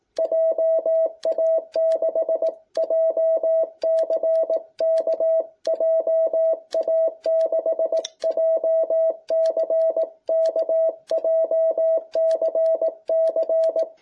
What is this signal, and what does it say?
Morse code, heard as a keyed tone, about 18 wpm, 660 Hz. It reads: JA6JCK JA6JCK JCC